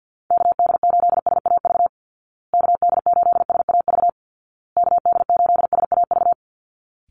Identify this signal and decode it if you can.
Morse code, heard as a keyed tone, about 50 wpm, 700 Hz. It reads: KD8SAV KD8SAV KD8SAV
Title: Call Signs Repeated 2x - 50wpm